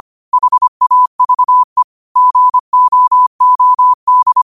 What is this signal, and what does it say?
Morse code, heard as a keyed tone, about 25 wpm, 1000 Hz. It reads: HAVE GOOD
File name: qotdfiles / 25wpm